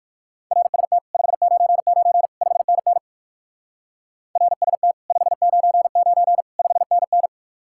Morse code, heard as a keyed tone, about 64 wpm, 700 Hz. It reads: RST 599 5NN RST 599 5NN